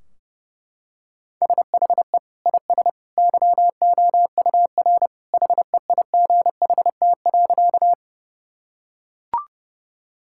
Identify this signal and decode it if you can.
Morse code, heard as a keyed tone, about 30 wpm, 700 Hz. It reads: SHE IS YOUR HEIGHT.
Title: Sentences from Top 1600 Words 30wpm